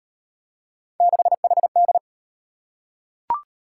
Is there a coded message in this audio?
Yes: Morse code from 1.0 to 2.0 s, heard as a keyed tone, about 35 wpm, 700 Hz. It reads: 6HD